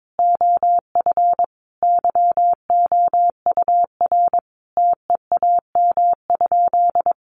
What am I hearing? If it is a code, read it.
Morse code, heard as a keyed tone, about 22 wpm, 700 Hz. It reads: OF YOUR TEAM?
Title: Sentences from Top 1500 Words 22wpm